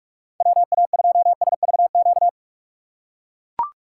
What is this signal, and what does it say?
Morse code, heard as a keyed tone, about 42 wpm, 700 Hz. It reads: WA2SVQ